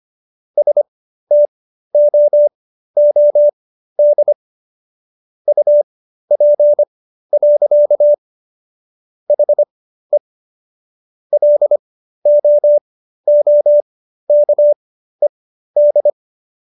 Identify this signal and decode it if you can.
Morse code, heard as a keyed tone, about 25 wpm, spaced as for 13 wpm, 590 Hz. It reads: STOOD UP. HE LOOKED